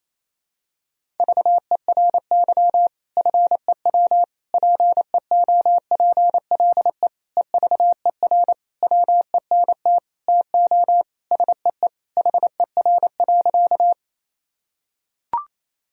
Morse code, heard as a keyed tone, about 28 wpm, 700 Hz. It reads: VERY FEW PEOPLE EVER WENT TO SEE HER.
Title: Sentences from Top 1000 Words - Review Part 1 - 28wpm